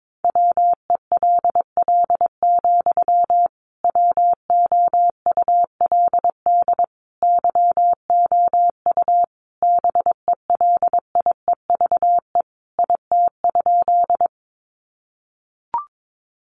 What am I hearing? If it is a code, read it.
Morse code, heard as a keyed tone, about 22 wpm, 700 Hz. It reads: WELL, WOULD YOU BELIEVE IT?
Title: Sentences from Top 1000 Words 22wpm